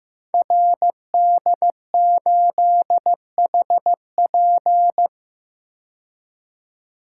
Morse code, heard as a keyed tone, about 15 wpm, 700 Hz. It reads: RD8HP
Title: Code Groups - 5 Letter:Numbers - 15wpm